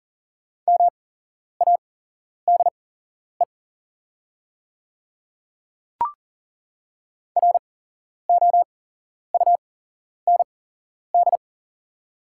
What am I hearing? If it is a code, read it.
Morse code, heard as a keyed tone, about 39 wpm, spaced as for 11 wpm, 700 Hz. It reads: MADE ROUND
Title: 8x Character Spacing - Top 200 Words - 40wpm